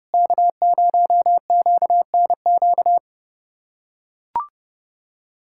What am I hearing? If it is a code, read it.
Morse code, heard as a keyed tone, about 30 wpm, 700 Hz. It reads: K0QNQ
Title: USA Callsign DE Callsign - 5 Characters - 30wpm